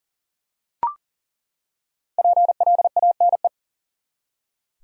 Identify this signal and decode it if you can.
Morse code, heard as a keyed tone, about 40 wpm, 700 Hz. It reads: PLANE
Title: Five Letter Words - 40wpm